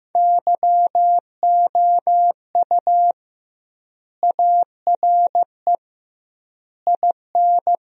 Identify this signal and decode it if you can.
Morse code, heard as a keyed tone, about 16 wpm, 700 Hz. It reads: YOU ARE IN
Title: 2x Word-Spacing - Sentences from Top 500 Words - 15wpm